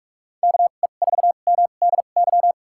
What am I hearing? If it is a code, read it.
Morse code, heard as a keyed tone, about 46 wpm, spaced as for 33 wpm, 690 Hz. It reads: KE4MDY
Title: USA Call Signs - 6 Characters - 2x Character Spacing - 45wpm